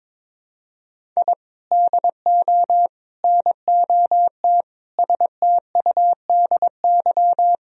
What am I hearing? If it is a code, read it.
Morse code, heard as a keyed tone, about 21 wpm, 700 Hz. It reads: I DO NOT STUDY
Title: Sentences from Top 1000 Words - Review Part 1 - 22wpm